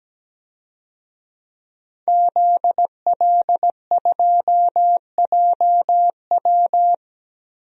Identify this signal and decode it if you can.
Morse code, heard as a keyed tone, about 17 wpm, 700 Hz. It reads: ZL2JW
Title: DXCC-100 Call Signs - 5 Characters - 17wpm